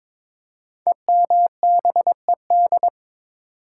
Morse code, heard as a keyed tone, about 22 wpm, 700 Hz. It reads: EMBED
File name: 5 Letter Words - First 3 Letters Given - 22wpm